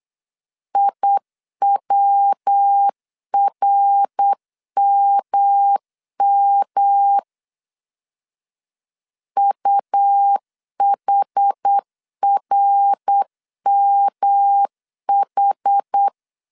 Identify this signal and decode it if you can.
Morse code, heard as a keyed tone, about 8 wpm, 800 Hz. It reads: IWRMM UHRMH
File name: CW / DET / 1 3 1500C S C 8wpm 4214